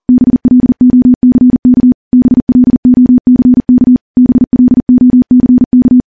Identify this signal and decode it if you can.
Morse code, heard as a keyed tone, about 40 wpm, 250 Hz. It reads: BLOCK BLOCK BLOCK
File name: morse / morse-training-40WPM